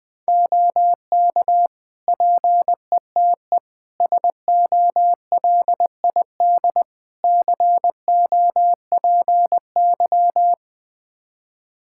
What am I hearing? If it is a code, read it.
Morse code, heard as a keyed tone, about 20 wpm, 700 Hz. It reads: OK PETE SOLID COPY
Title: Practice QSO - Exchange 3 - More Info - 20wpm